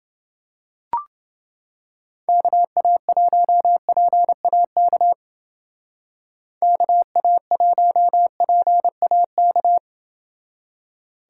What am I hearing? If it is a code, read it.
Morse code, heard as a keyed tone, about 28 wpm, 700 Hz. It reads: KA1PAK KA1PAK